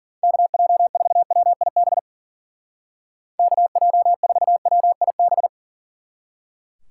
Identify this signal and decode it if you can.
Morse code, heard as a keyed tone, about 48 wpm, 700 Hz. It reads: KJ4WIB KJ4WIB
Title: Speed Racing - USA 2x3 Call Signs 40wpm